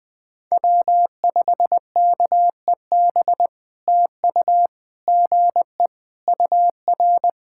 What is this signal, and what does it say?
Morse code, heard as a keyed tone, about 18 wpm, 700 Hz. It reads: W5KEB TU GE UR
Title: POTA QSO Practice - 20wpm